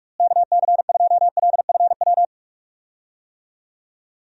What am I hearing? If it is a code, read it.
Morse code, heard as a keyed tone, about 41 wpm, 700 Hz. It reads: KC2LFW
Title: USA Call Signs - 6 Characters - Repeated 2x - 45wpm